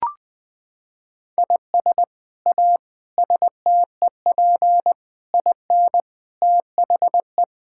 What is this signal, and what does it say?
Morse code, heard as a keyed tone, about 20 wpm, 700 Hz. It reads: IS A STEP IN THE
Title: Sets of 8 Words - 20wpm